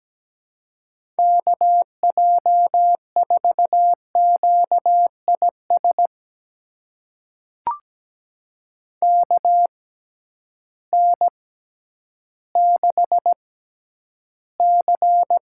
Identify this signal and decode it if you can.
Morse code, heard as a keyed tone, about 17 wpm, spaced as for 10 wpm, 700 Hz. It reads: KJ4QIS KN6C
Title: USA Call Signs - 6 Characters - 6x Character Spacing - 17wpm